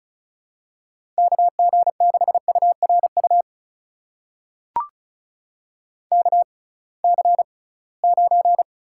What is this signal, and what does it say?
Morse code, heard as a keyed tone, about 36 wpm, spaced as for 23 wpm, 700 Hz. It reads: KG6URU KC9
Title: USA Call Signs - 6 Characters - 6x Character Spacing - 35wpm